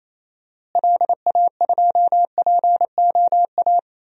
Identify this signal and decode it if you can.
Morse code, heard as a keyed tone, about 28 wpm, 700 Hz. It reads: LA2POA
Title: DXCC-100 Call Signs - 6 Characters - Repeated 2x - 28wpm